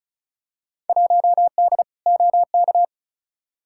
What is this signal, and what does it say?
Morse code, heard as a keyed tone, about 36 wpm, 700 Hz. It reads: 1D OK
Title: Field Day Exchanges - 35wpm